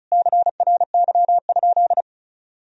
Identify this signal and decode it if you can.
Morse code, heard as a keyed tone, about 36 wpm, 700 Hz. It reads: CRY?
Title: Sentences from Top 1000 Words - Review Part 2 - 35wpm